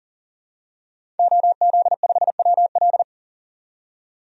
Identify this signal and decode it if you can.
Morse code, heard as a keyed tone, about 40 wpm, 700 Hz. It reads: OZ5WL
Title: DXCC-100 Call Signs - 5 Characters - 40wpm